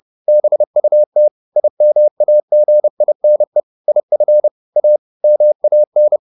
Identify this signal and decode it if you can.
Morse code, heard as a keyed tone, about 30 wpm, 600 Hz. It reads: BUT IMAGINE IF A MAN